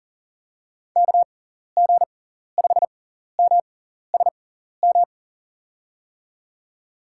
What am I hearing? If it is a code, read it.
Morse code, heard as a keyed tone, about 38 wpm, spaced as for 14 wpm, 700 Hz. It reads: KG5MSM
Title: USA Call Signs - 6 Characters - 6x Character Spacing - 40wpm